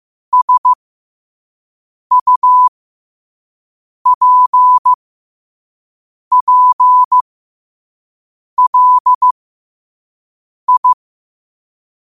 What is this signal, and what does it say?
Morse code, heard as a keyed tone, about 15 wpm, spaced as for 5 wpm, 1000 Hz. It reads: SUPPLI
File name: qotdfiles / 5wpm